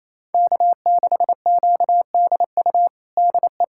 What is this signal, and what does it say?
Morse code, heard as a keyed tone, about 28 wpm, 700 Hz. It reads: K6QDU DE
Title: USA Callsign DE Callsign 28wpm